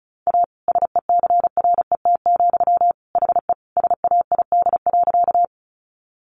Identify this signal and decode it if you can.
Morse code, heard as a keyed tone, about 34 wpm, 700 Hz. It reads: A SECRET, HE SAID.
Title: Sentences from Top 1200 Words 35wpm